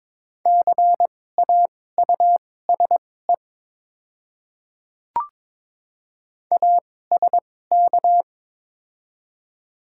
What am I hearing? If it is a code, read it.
Morse code, heard as a keyed tone, about 22 wpm, spaced as for 16 wpm, 700 Hz. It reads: CAUSE ASK
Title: 2x Character Spacing - Top 200 Words - 22wpm